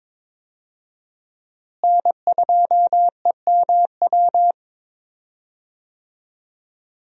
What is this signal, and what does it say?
Morse code, heard as a keyed tone, about 22 wpm, 700 Hz. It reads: N2EMW
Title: USA Call Signs - 5 Characters - Repeated 2x - 22wpm